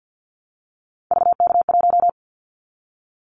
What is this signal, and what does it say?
Morse code, heard as a keyed tone, about 50 wpm, 700 Hz. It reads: VK1
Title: Callsign Prefixes for Top 100 DXCC - 50wpm